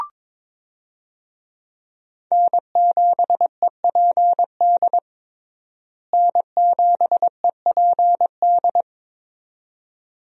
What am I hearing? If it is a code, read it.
Morse code, heard as a keyed tone, about 22 wpm, 700 Hz. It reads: N7EPD N7EPD
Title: USA Call Signs - 5 Characters - Repeated 2x - 22wpm